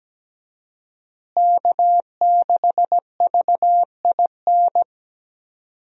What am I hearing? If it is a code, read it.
Morse code, heard as a keyed tone, about 17 wpm, 700 Hz. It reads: K6VIN
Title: USA Call Signs - 5 Characters - Repeated 3x - 17wpm